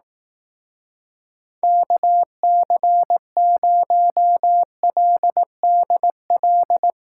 Morse code, heard as a keyed tone, about 18 wpm, 700 Hz. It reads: KC0LDL
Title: Speed Racing - USA Call Signs 15wpm